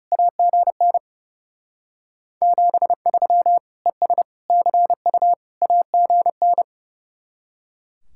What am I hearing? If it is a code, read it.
Morse code, heard as a keyed tone, about 35 wpm, 700 Hz. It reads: AGN 73 ES CU AGN
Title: Speed-Racing Common QSO Phrases - 30wpm